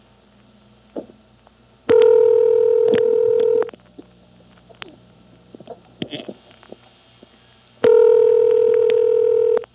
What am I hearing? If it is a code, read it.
Ringback tone (North America), 440 + 480 Hz.
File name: MoreheadCityNCNoTest